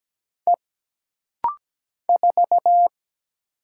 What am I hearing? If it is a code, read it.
Morse code, heard as a keyed tone, about 17 wpm, 700 Hz. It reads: E 4
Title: Single Letter-Number Alt - Rapid-Fire 17wpm